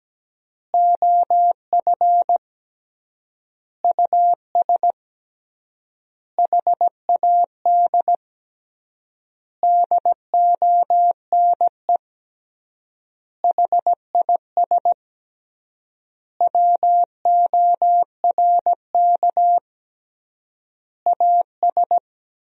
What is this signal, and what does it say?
Morse code, heard as a keyed tone, about 17 wpm, 700 Hz. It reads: OF US HAD DONE HIS WORK AS
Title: 3x Word-Spacing - Sentences from Top 500 Words - 17wpm